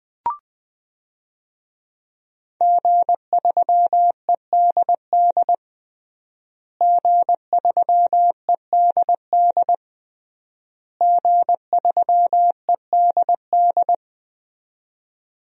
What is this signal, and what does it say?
Morse code, heard as a keyed tone, about 20 wpm, 700 Hz. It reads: G3EDD G3EDD G3EDD
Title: DXCC-100 Call Signs - 5 Characters - Repeated 3x - 20wpm